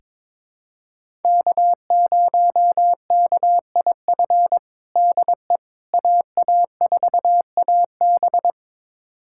Morse code, heard as a keyed tone, about 22 wpm, 700 Hz. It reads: K0KIF DE AA4AB
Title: USA Callsign DE Callsign - 5 Characters - 22wpm